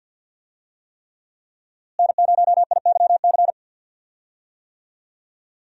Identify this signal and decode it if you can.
Morse code, heard as a keyed tone, about 46 wpm, 700 Hz. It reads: N0IYC
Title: USA Call Signs - 5 Characters - Repeated 3x - 50wpm